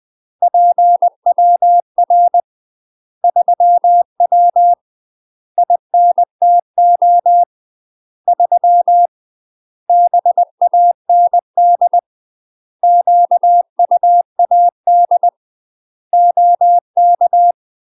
Morse code, heard as a keyed tone, about 20 wpm, 700 Hz. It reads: PWR 3W INTO 3 BAND QUAD OK
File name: UK MASTER QSOs TRAINING / 20WPM / UK MASTER QSOs 0044